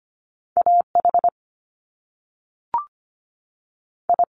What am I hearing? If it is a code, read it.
Morse code, heard as a keyed tone, about 25 wpm, 700 Hz. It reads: A H I
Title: Top Two Letter Combinations in Words 25wpm